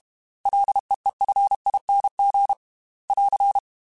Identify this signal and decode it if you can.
Morse code, heard as a keyed tone, about 32 wpm, 800 Hz. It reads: LEEFING <AR>
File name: F and L / F and L mp3 / F and L Prefixes 3x2 Twice @30